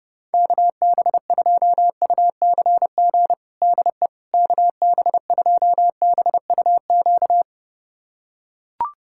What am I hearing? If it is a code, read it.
Morse code, heard as a keyed tone, about 32 wpm, 700 Hz. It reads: KB2UCG DE KB2BUQ
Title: USA Callsign DE Callsign - 6 Characters - 30wpm